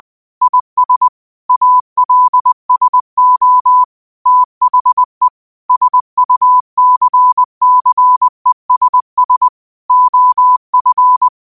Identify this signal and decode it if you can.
Morse code, heard as a keyed tone, about 20 wpm, 1000 Hz. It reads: IS ALSO THE SUCCESS OF
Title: 20 WPM morse code quotes for Thu, 29 Feb 2024